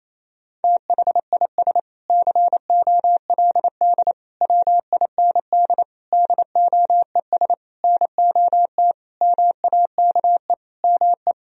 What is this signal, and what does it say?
Morse code, heard as a keyed tone, about 28 wpm, 700 Hz. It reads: THIS COLD WIND DOES NOT MAKE ME
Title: Sentences from Top 1000 Words - Review Part 2 - 28wpm